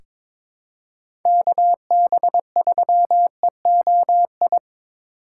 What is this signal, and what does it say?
Morse code, heard as a keyed tone, about 21 wpm, 700 Hz. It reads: KB3EOI
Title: USA Call Signs - 6 Characters - 3x Character Spacing - 22wpm